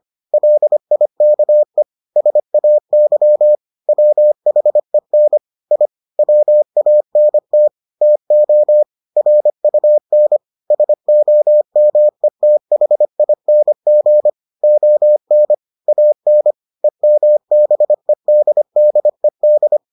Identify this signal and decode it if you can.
Morse code, heard as a keyed tone, about 25 wpm, 600 Hz. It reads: LIKE SAY WHEN I WANT TO RUN SOMETHING ON AN EMBEDDED